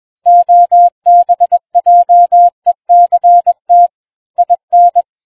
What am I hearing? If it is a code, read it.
Morse code, heard as a keyed tone, about 21 wpm, 700 Hz. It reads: OBJECT IN